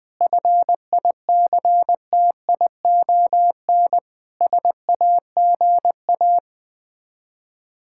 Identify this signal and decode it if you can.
Morse code, heard as a keyed tone, about 20 wpm, 690 Hz. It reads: FICTION SAGA
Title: Difficult Words with Hints - 20wpm